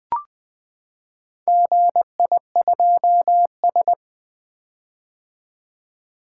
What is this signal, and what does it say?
Morse code, heard as a keyed tone, about 20 wpm, 700 Hz. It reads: GI2S
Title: Four Letter-Number Combinations - 20wpm